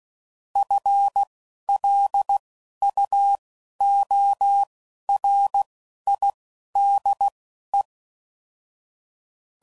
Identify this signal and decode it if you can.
Morse code, heard as a keyed tone, about 16 wpm, spaced as for 11 wpm, 800 Hz. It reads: FLUORIDE